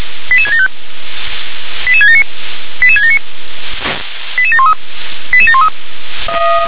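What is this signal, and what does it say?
Five-tone selective call (ZVEI), heard as quick runs of five tones, tones of about 80 ms. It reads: 88675 88679 88679 88713 88713
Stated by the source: ZVEI five-tone selective call received by SDR.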